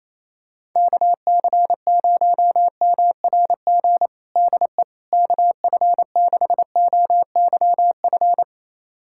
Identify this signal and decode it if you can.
Morse code, heard as a keyed tone, about 28 wpm, 700 Hz. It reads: KC0MRG DE KF6OYF
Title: USA Callsign DE Callsign - 6 Characters - 28wpm